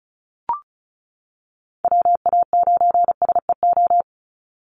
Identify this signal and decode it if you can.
Morse code, heard as a keyed tone, about 34 wpm, 700 Hz. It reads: WA9SEO
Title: Call Signs - 35wpm